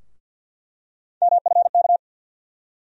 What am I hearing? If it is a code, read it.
Morse code, heard as a keyed tone, about 49 wpm, 700 Hz. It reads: MFK
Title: USA Call Sign Postfixes - 3 Letters 50wpm